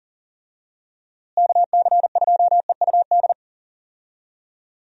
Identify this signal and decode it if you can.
Morse code, heard as a keyed tone, about 40 wpm, 700 Hz. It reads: KC2EUD
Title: USA Call Signs - 6 Characters - Repeated 2x - 40wpm